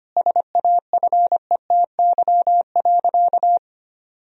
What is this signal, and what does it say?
Morse code, heard as a keyed tone, about 25 wpm, 700 Hz. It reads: SAFETY.
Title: Sentences from Top 1000 Words - Review Part 4 - 25wpm